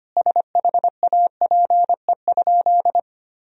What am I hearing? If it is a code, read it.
Morse code, heard as a keyed tone, about 25 wpm, 700 Hz. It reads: SHAPE?
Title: Sentences from Top 1000 Words - Review Part 3 - 25wpm